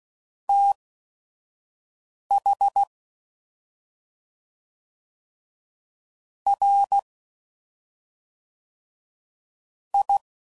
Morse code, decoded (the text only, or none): TH R I